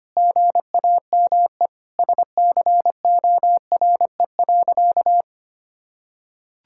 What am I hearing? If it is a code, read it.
Morse code, heard as a keyed tone, about 25 wpm, 700 Hz. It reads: GAME SCORE.
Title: Sentences from Top 1600 Words 25wpm